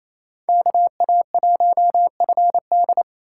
Morse code, heard as a keyed tone, about 36 wpm, 700 Hz. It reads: KA1FD